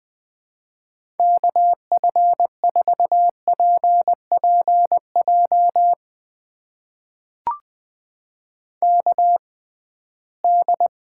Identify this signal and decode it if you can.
Morse code, heard as a keyed tone, about 20 wpm, spaced as for 15 wpm, 700 Hz. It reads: KF4PPJ KD
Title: USA Call Signs - 6x Character Spacing - 20wpm